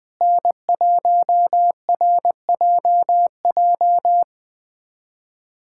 Morse code, heard as a keyed tone, about 20 wpm, 700 Hz. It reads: N1RJJ
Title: USA Callsign DE Callsign - 5 Characters - 20wpm